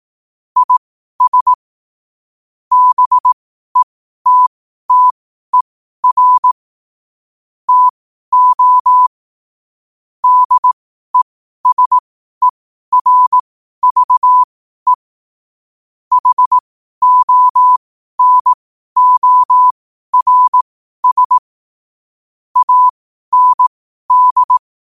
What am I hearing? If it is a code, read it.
Morse code, heard as a keyed tone, about 18 wpm, spaced as for 12 wpm, 1000 Hz. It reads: IS BETTER TO DESERVE HONORS AND
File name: qotdfiles / 12wpm